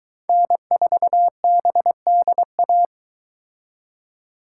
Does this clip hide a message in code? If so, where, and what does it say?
Yes: Morse code from 0.3 to 2.9 s, heard as a keyed tone, about 22 wpm, 700 Hz. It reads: N4BDA